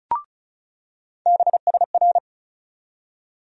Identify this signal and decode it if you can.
Morse code, heard as a keyed tone, about 36 wpm, 700 Hz. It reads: BSR
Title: USA Call Sign Postfixes - 3 Letters 35wpm